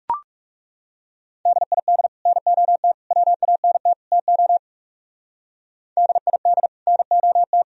Morse code, heard as a keyed tone, about 35 wpm, 700 Hz. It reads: DID NOT WANT TO DID NOT
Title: Speed-Racing - Sets of 4 Words 30wpm